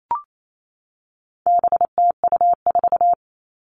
Morse code, heard as a keyed tone, about 28 wpm, 700 Hz. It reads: BTU4
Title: All Cut Numbers - 4-Digit 28wpm